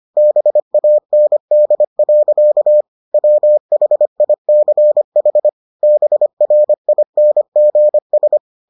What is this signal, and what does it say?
Morse code, heard as a keyed tone, about 25 wpm, 600 Hz. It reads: BAND. WHICH BRINGS